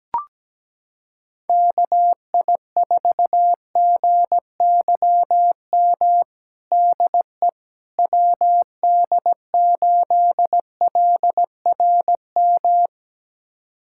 Morse code, heard as a keyed tone, about 17 wpm, 700 Hz. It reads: KI4GYM DE WD8LRM
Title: USA Callsign DE Callsign - 6 Characters - 17wpm